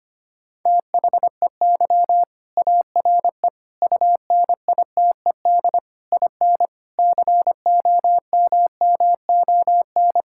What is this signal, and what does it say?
Morse code, heard as a keyed tone, about 25 wpm, 700 Hz. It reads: THEY ARE UNITED IN COMMON
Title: Sentences from Top 1000 Words - Review Part 5 - 25wpm